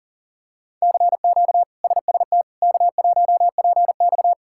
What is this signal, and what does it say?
Morse code, heard as a keyed tone, about 40 wpm, 700 Hz. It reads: CQ SST K1PX
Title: K1USN SST QSO Practice - 40wpm